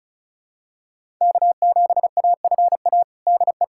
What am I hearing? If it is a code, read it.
Morse code, heard as a keyed tone, about 36 wpm, 700 Hz. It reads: K7AFA DE
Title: USA Callsign DE Callsign - 5 Characters - 35wpm